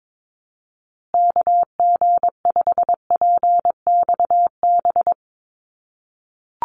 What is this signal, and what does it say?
Morse code, heard as a keyed tone, about 22 wpm, 700 Hz. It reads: KG5PXB
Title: USA Call Signs Repeated 2x 22wpm